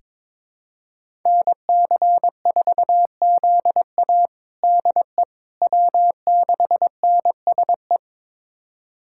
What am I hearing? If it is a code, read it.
Morse code, heard as a keyed tone, about 22 wpm, 700 Hz. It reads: NC4ZA DE W6NSE
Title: USA Callsign DE Callsign - 5 Characters - 22wpm